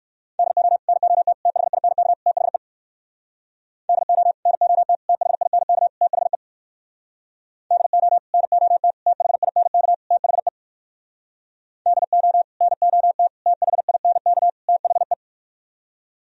Morse code, heard as a keyed tone, about 58 wpm, 700 Hz. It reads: DO NOT THINK THE DO NOT THINK THE DO NOT THINK THE DO NOT THINK THE